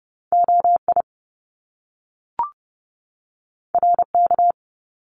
Morse code, heard as a keyed tone, about 30 wpm, 700 Hz. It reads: OI RK
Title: Top Two Letter Combinations in Words 30wpm